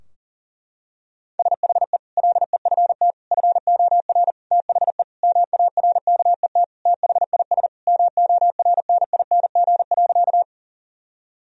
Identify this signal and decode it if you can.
Morse code, heard as a keyed tone, about 40 wpm, 700 Hz. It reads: SHE LEFT FOR THE MARKET THIS MORNING.
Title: Sentences from Top 1000 Words - Review Part 3 - 40wpm